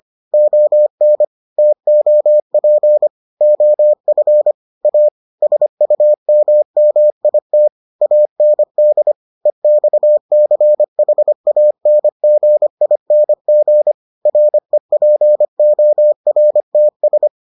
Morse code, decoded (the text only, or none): ON TOP OF A SUMMIT AND EXCHANGING REPORTS